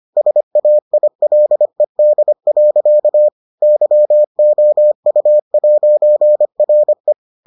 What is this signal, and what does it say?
Morse code, heard as a keyed tone, about 25 wpm, 600 Hz. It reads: SAILED. YOU'RE